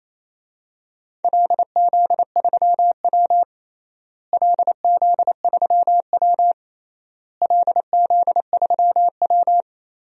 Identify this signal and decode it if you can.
Morse code, heard as a keyed tone, about 28 wpm, 700 Hz. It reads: LZ3W LZ3W LZ3W
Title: DXCC-100 Call Signs - 4 Characters - Repeated 3x - 28wpm